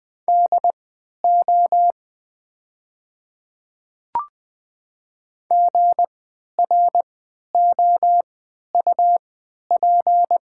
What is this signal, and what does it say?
Morse code, heard as a keyed tone, about 20 wpm, spaced as for 11 wpm, 700 Hz. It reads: DO GROUP
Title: 3x Character Spacing - Top 500 Words - 20wpm